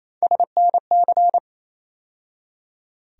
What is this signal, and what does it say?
Morse code, heard as a keyed tone, about 28 wpm, 700 Hz. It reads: SNC
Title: USA Call Sign Postfixes - 3 Letters 28wpm